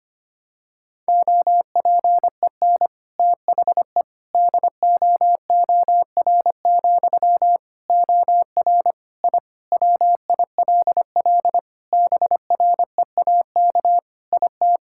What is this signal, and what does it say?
Morse code, heard as a keyed tone, about 25 wpm, 700 Hz. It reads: OPEN THE DOOR, OR I WILL BREAK IT
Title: Sentences from Top 1000 Words - Review Part 6 - 25wpm